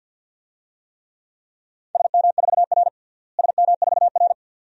Morse code, heard as a keyed tone, about 52 wpm, 700 Hz. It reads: SM4R SM4R
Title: DXCC-100 Call Signs - 4 Characters - Repeated 2x - 50wpm